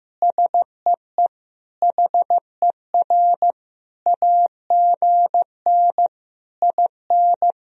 Morse code, heard as a keyed tone, about 15 wpm, 700 Hz. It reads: SEE HER AGN IN
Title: Sentences from Top 200 Words and Abbreviations 15wpm